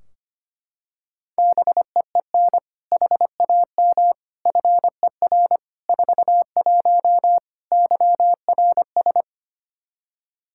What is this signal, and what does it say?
Morse code, heard as a keyed tone, about 25 wpm, 700 Hz. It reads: BEEN HAM FER 41 YRS
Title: K1USN SST QSO Practice 25wpm